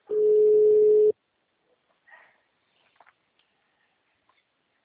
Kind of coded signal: telephone call progress tone

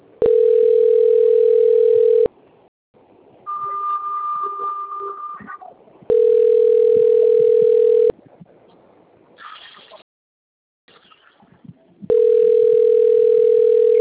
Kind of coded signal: telephone call progress tone

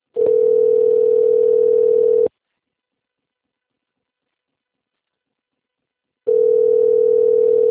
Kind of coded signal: telephone call progress tone

